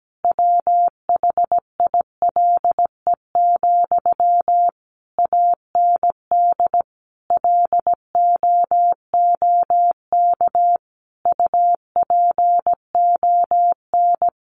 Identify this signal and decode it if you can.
Morse code, heard as a keyed tone, about 17 wpm, 700 Hz. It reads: WHILE, AND LOOK UPON
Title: Sentences from Top 1100 Words 17wpm